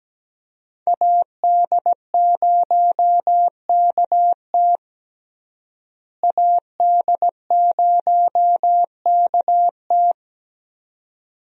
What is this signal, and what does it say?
Morse code, heard as a keyed tone, about 17 wpm, 700 Hz. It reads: AD0KT AD0KT
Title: USA Call Signs - 5 Characters - Repeated 2x - 17wpm